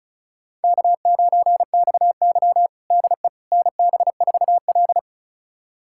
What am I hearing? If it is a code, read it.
Morse code, heard as a keyed tone, about 36 wpm, 700 Hz. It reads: K9XY DE NB4L
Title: USA Callsign DE Callsign - 4 Characters - 35wpm